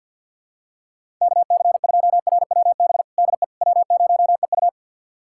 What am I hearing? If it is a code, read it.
Morse code, heard as a keyed tone, about 51 wpm, 700 Hz. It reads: KC2RWB DE W0EU
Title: USA Callsign DE Callsign 50wpm